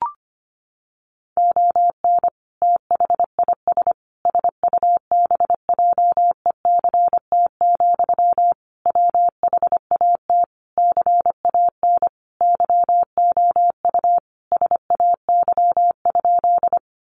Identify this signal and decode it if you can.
Morse code, heard as a keyed tone, about 25 wpm, 700 Hz. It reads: ON THIS SUBJECT, WHAT CAN YOU SAY?
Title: Sentences from Top 800 Words 25wpm